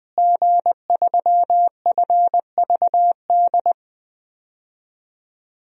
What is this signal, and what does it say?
Morse code, heard as a keyed tone, about 20 wpm, 700 Hz. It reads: G3FVD
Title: DXCC-100 Call Signs - 5 Characters - Repeated 3x - 20wpm